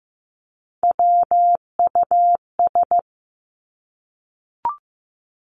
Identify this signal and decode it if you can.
Morse code, heard as a keyed tone, about 15 wpm, 700 Hz. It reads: WUS
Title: Three Challenging Letter-Number Combinations 15wpm